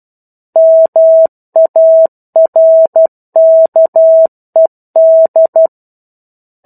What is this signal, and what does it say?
Morse code, heard as a keyed tone, about 12 wpm, 650 Hz. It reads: MARKED
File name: 01-Chapter-01 12 WPM